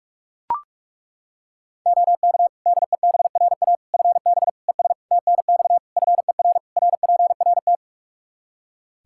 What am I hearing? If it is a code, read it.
Morse code, heard as a keyed tone, about 41 wpm, 700 Hz. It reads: OK DEBRA FB ES TNX FER RPRT
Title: Practice QSO - Exchange 2 - Info - 45wpm